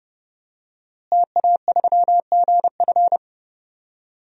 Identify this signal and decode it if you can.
Morse code, heard as a keyed tone, about 32 wpm, 700 Hz. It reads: TA3GF
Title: DXCC-100 Call Signs - 5 Characters - Repeated 3x - 30wpm